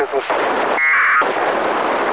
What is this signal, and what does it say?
Packet radio (AX.25, as in APRS), with speech.
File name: ho-68-12-24-2009